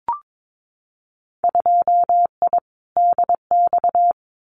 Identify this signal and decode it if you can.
Morse code, heard as a keyed tone, about 22 wpm, 700 Hz. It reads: 2I DX
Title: Winter Field Day Exchanges 22wpm